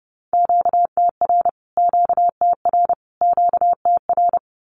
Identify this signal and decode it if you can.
Morse code, heard as a keyed tone, about 30 wpm, 700 Hz. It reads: QTR QTR QTR
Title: Q-Codes Repeated 3x 30wpm